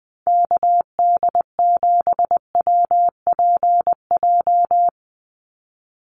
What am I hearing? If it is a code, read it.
Morse code, heard as a keyed tone, about 19 wpm, 700 Hz. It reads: KD7WPJ
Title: NA SOTA Activator Callsigns 20wpm